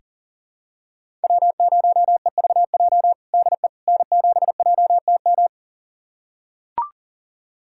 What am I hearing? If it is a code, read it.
Morse code, heard as a keyed tone, about 40 wpm, 700 Hz. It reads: W0EVJ DE N7JTM
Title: USA Callsign DE Callsign - 5 Characters - 40wpm